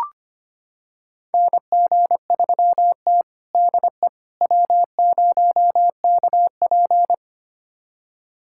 Morse code, heard as a keyed tone, about 25 wpm, 700 Hz. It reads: NG3T DE W0KP
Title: USA Callsign DE Callsign - 4 Characters - 25wpm